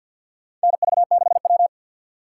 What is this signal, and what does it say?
Morse code, heard as a keyed tone, about 50 wpm, 700 Hz. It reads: NV6W
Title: USA Call Signs - 4 Characters - 3x Character Spacing - 50wpm